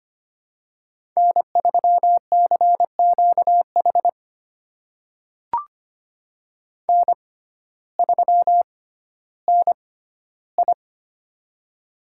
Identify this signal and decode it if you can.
Morse code, heard as a keyed tone, about 25 wpm, spaced as for 14 wpm, 700 Hz. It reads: N3CQH N3NI
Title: USA Call Signs - 6x Character Spacing - 25wpm